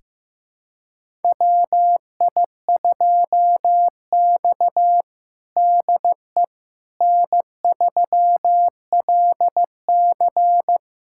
Morse code, heard as a keyed tone, about 16 wpm, 700 Hz. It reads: WI2X DE N3LC
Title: USA Callsign DE Callsign - 4 Characters - 15wpm